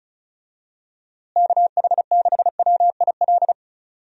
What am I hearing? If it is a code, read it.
Morse code, heard as a keyed tone, about 36 wpm, 700 Hz. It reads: KH6WIL
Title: USA Call Signs - 6 Characters - Repeated 3x - 35wpm